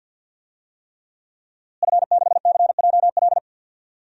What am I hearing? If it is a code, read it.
Morse code, heard as a keyed tone, about 48 wpm, 700 Hz. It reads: F6CJL
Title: DXCC-100 Call Signs - 5 Characters - Repeated 3x - 50wpm